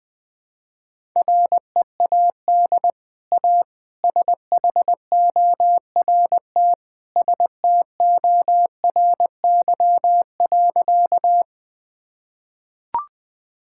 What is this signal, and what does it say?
Morse code, heard as a keyed tone, about 21 wpm, 700 Hz. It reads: READ A SHORT STORY.
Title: Easy Sentences with Hints - 20wpm